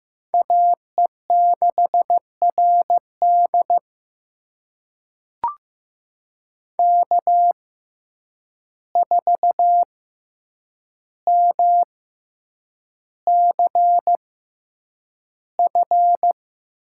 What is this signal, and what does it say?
Morse code, heard as a keyed tone, about 15 wpm, spaced as for 8 wpm, 700 Hz. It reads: AE6RD K4MCF
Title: USA Call Signs - 5 Characters - 6x Character Spacing - 15wpm